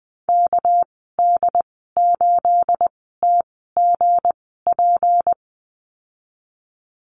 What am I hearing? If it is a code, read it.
Morse code, heard as a keyed tone, about 20 wpm, spaced as for 15 wpm, 700 Hz. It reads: KD8TGP
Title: 231 Lesson 37g - Letter x - Call Signs 20wpm